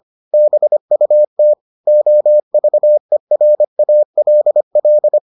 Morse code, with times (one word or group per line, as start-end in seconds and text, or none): 0.3-1.5 BUT
1.9-5.2 OVERALL